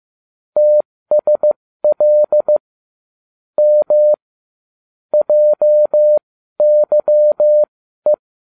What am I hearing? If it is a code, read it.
Morse code, heard as a keyed tone, about 15 wpm, spaced as for 12 wpm, 600 Hz. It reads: TSL M JYE